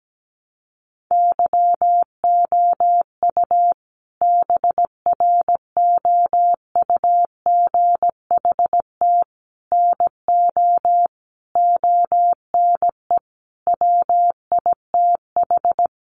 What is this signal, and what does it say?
Morse code, heard as a keyed tone, about 17 wpm, 700 Hz. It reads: YOU BROUGHT NO ONE WITH
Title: Sentences from Top 500 Words 17wpm